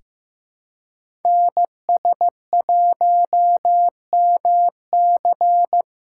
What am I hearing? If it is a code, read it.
Morse code, heard as a keyed tone, about 15 wpm, 700 Hz. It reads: NS1MC
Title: USA Call Signs - 5 Characters - 6x Character Spacing - 15wpm